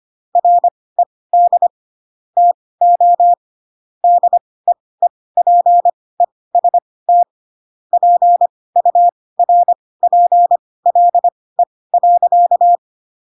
Morse code, heard as a keyed tone, about 25 wpm, spaced as for 18 wpm, 700 Hz. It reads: RED TO DEEPEST PURPLE.
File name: Pinocchio / Pinocchio ch1 12 25 ik1hge